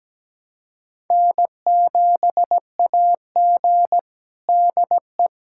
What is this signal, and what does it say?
Morse code, heard as a keyed tone, about 17 wpm, 700 Hz. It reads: N7AG DE
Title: USA Callsign DE Callsign - 4 Characters - 17wpm